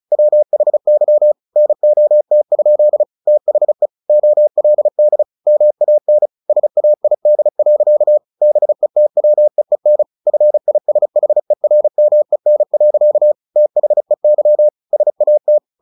Morse code, heard as a keyed tone, about 35 wpm, 600 Hz. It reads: WHY NOT? THE OLD MAN SAID. BETWEEN FISHERMEN. THEY SAT
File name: the old man and the sea / the old man and the sea 1 35wpm